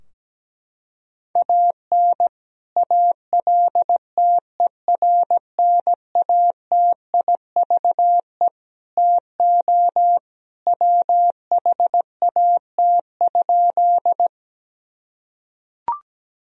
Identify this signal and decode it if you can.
Morse code, heard as a keyed tone, about 17 wpm, 700 Hz. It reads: AN ALTERNATIVE TO WHAT?
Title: Sentences from Top 1000 Words - Review Part 4 - 17wpm